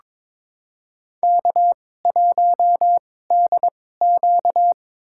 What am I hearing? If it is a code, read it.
Morse code, heard as a keyed tone, about 22 wpm, spaced as for 16 wpm, 700 Hz. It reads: K1DQ
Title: USA Call Signs - 2x Character Spacing - 22wpm